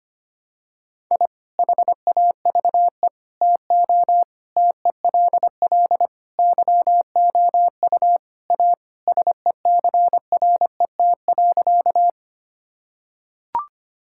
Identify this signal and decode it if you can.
Morse code, heard as a keyed tone, about 25 wpm, 700 Hz. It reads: I HAVE TO TELL YOU A SECRET.
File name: practice / 25wpm / Sentences with 3-Word Phrases - 25wpm